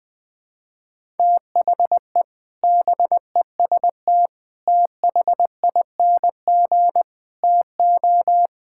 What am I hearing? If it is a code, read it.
Morse code, heard as a keyed tone, about 20 wpm, 700 Hz. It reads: THE BEST THING TO
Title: Sentences with 7-Word Phrases - Spoken Beginning - 20wpm